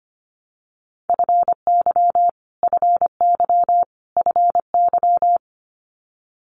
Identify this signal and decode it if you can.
Morse code, heard as a keyed tone, about 25 wpm, 700 Hz. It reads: FY FY FY
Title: Multi-letter Phonemes Repeated 3x 25wpm